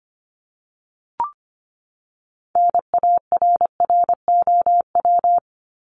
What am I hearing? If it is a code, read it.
Morse code, heard as a keyed tone, about 25 wpm, 700 Hz. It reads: NARROW
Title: Top 1501-1750 Words 25wpm